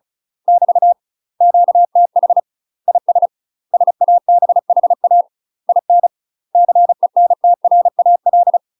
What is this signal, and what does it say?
Morse code, heard as a keyed tone, about 34 wpm, 700 Hz. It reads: <BT> QTH IS SABHA IN CENTRAL
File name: UK MASTER QSOs TRAINING / 35WPM / UK MASTER QSOs 0042